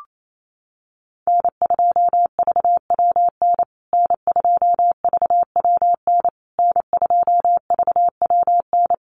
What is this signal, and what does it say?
Morse code, heard as a keyed tone, about 28 wpm, 700 Hz. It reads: N2VWN N2VWN N2VWN
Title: USA Call Signs Repeated 3x 28wpm